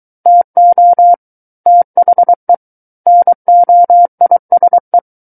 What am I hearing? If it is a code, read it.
Morse code, heard as a keyed tone, about 23 wpm, 700 Hz. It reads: TO THE NOISE
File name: Book One / 14-Chapter 14 - 23 WPM